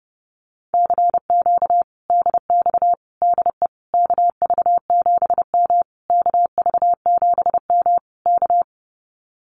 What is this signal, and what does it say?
Morse code, heard as a keyed tone, about 30 wpm, 700 Hz. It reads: CQ DX DE KV7M KV7M K
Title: Common QSO Phrases Encore 30wpm